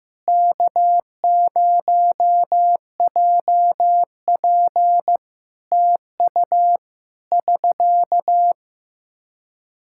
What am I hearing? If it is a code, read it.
Morse code, heard as a keyed tone, about 15 wpm, 700 Hz. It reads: K0JP TU <SK>
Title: Practice QSO - Exchange 4 - Ending - 15wpm